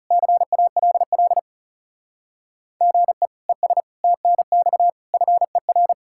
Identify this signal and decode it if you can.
Morse code, heard as a keyed tone, about 40 wpm, 700 Hz. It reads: CALL GE ES TNX FER